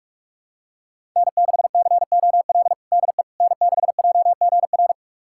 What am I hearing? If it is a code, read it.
Morse code, heard as a keyed tone, about 46 wpm, 700 Hz. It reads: N6COL DE N6JGR
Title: USA Callsign DE Callsign - 5 Characters - 45wpm